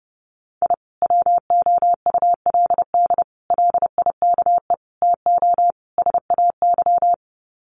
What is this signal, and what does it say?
Morse code, heard as a keyed tone, about 30 wpm, 700 Hz. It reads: I WOULD LIKE TO SAY
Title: Sets of 5 Words Encore2 30wpm